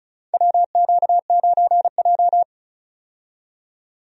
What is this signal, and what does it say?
Morse code, heard as a keyed tone, about 36 wpm, 700 Hz. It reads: WQ9J
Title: USA Call Signs - 4 Characters - 35wpm